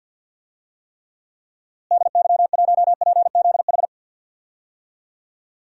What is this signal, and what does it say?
Morse code, heard as a keyed tone, about 50 wpm, 700 Hz. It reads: DY1PZH
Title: DXCC-100 Call Signs - 6 Characters - Repeated 3x - 50wpm